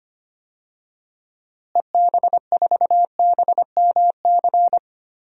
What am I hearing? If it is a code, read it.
Morse code, heard as a keyed tone, about 25 wpm, 700 Hz. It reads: EB4BMC
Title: DXCC-100 Call Signs - 6 Characters - Repeated 3x - 25wpm